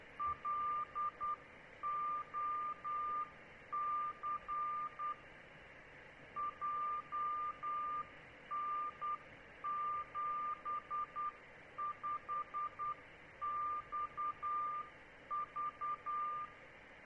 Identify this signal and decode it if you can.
Morse code, heard as a keyed tone, about 10 wpm, 1190 Hz, with noise. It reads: LOC JN75XV